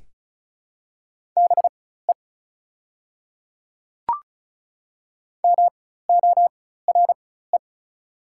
Morse code, heard as a keyed tone, about 36 wpm, spaced as for 16 wpm, 700 Hz. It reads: BE MORE
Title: 4x Character Spacing - Top 100 Words - 35wpm